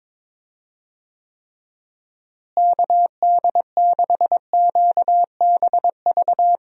Morse code, heard as a keyed tone, about 22 wpm, 700 Hz. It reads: KD6QBV
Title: USA Call Signs - 6 Characters - Repeated 2x - 22wpm